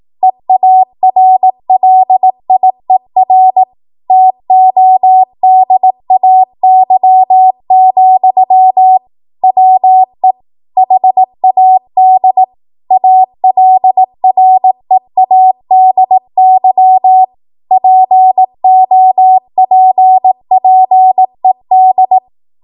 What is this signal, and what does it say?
Morse code, heard as a keyed tone, about 18 wpm, 750 Hz. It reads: EARLIER TODAY, WE HAD ALREADY POPPED
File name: Archive / 18 WPM / 18WPM